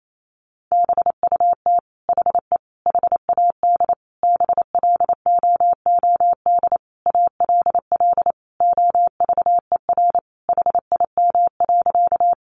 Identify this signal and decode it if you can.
Morse code, heard as a keyed tone, about 28 wpm, 700 Hz. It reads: BUT HE HAD BLOOD ALL OVER HIM.
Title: Sentences from Top 1100 Words 28wpm